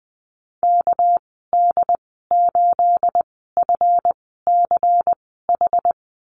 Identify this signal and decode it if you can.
Morse code, heard as a keyed tone, about 20 wpm, spaced as for 15 wpm, 700 Hz. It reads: KD8FCH
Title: 213 Lesson 34g - Number 8 - Call Signs 20wpm